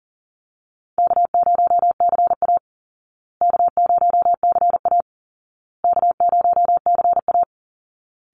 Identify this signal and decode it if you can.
Morse code, heard as a keyed tone, about 40 wpm, 700 Hz. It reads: K0CA K0CA K0CA
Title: Call Signs Repeated 2x - 40wpm